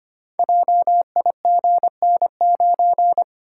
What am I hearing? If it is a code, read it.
Morse code, heard as a keyed tone, about 25 wpm, 700 Hz. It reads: JIGN9
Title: Code Groups - 5 Letter:Numbers - 25wpm